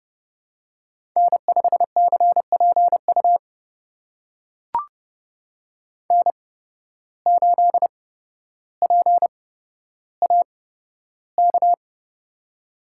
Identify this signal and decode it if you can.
Morse code, heard as a keyed tone, about 30 wpm, spaced as for 13 wpm, 700 Hz. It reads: N5CPU N8PAK
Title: USA Call Signs - 5 Characters - 8x Character Spacing - 30wpm